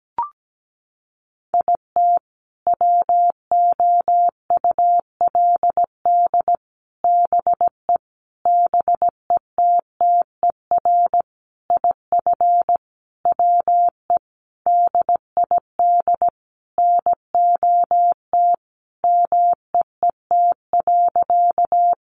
Morse code, heard as a keyed tone, about 17 wpm, 700 Hz. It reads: IT WOULD BE BETTER IF WE DID NOT MEET.
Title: Sentences from Top 1100 Words 17wpm